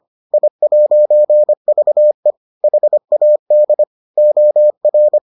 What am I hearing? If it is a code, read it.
Morse code, heard as a keyed tone, about 25 wpm, 600 Hz. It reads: I'VE HAD OR